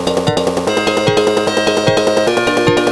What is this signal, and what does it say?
There is no code.